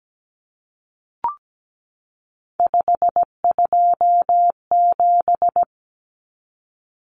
Morse code, heard as a keyed tone, about 17 wpm, 700 Hz. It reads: 527
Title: Numbers - 3-Digit 17wpm